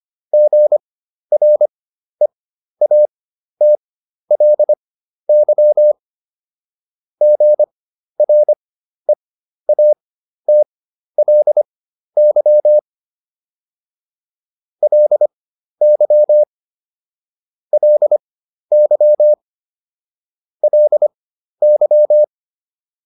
Morse code, heard as a keyed tone, about 25 wpm, spaced as for 12 wpm, 600 Hz. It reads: GREATLY GREATLY LY LY LY